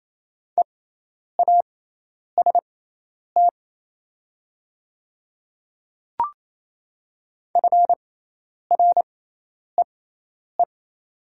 Morse code, heard as a keyed tone, about 28 wpm, spaced as for 10 wpm, 700 Hz. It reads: EAST FREE
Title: 6x Character Spacing - Top 500 Words - 28wpm